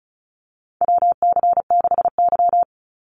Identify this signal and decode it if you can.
Morse code, heard as a keyed tone, about 34 wpm, 700 Hz. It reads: WC6Y
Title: Call Signs Repeated 2x - 35wpm